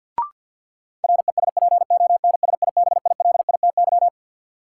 Morse code, heard as a keyed tone, about 47 wpm, 700 Hz. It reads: RESPONSIBILITY
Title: Sentences from Top 1200 Words - New Word Drill - 50wpm